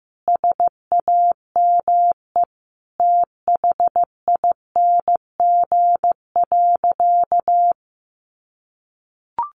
Morse code, heard as a keyed tone, about 15 wpm, 700 Hz. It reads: SAME THING.
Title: Sentences from Top 1300 Words 15wpm